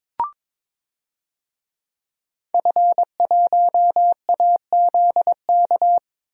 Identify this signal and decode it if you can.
Morse code, heard as a keyed tone, about 22 wpm, 700 Hz. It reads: F1AZK
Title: DXCC-100 Call Signs - 5 Characters - Repeated 2x - 22wpm